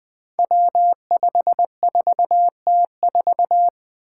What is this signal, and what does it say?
Morse code, heard as a keyed tone, about 20 wpm, 700 Hz. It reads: W54T4
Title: Five Challenging Letter-Number Combinations 20wpm